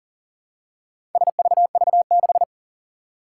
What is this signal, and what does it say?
Morse code, heard as a keyed tone, about 42 wpm, 700 Hz. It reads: SVV6
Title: Four Challenging Letter-Number Combinations 40wpm